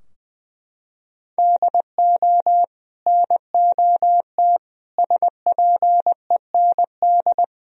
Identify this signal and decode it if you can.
Morse code, heard as a keyed tone, about 20 wpm, 700 Hz. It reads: DO NOT SPEND
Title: Sentences from Top 1000 Words - Review Part 4 - 20wpm